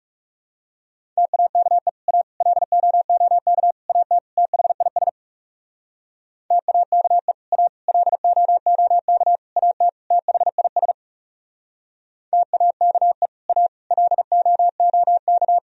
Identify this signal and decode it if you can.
Morse code, heard as a keyed tone, about 40 wpm, 700 Hz. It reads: TAKE A LOOK AT THIS TAKE A LOOK AT THIS TAKE A LOOK